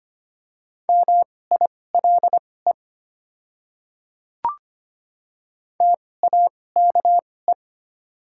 Morse code, heard as a keyed tone, about 25 wpm, spaced as for 18 wpm, 700 Hz. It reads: MILE TAKE
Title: 2x Character Spacing - Top 300 Words - 25wpm